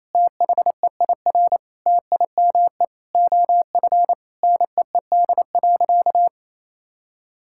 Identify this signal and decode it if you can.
Morse code, heard as a keyed tone, about 27 wpm, 700 Hz. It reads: THEIR TIME OF NEED.
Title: Sentences from Top 1000 Words - Review Part 5 - 28wpm